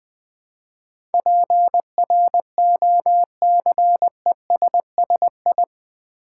Morse code, heard as a keyed tone, about 20 wpm, 700 Hz. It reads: PROCESSI
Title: Build A Word - 4 Letters At A Time - 10-Letter Words - 20wpm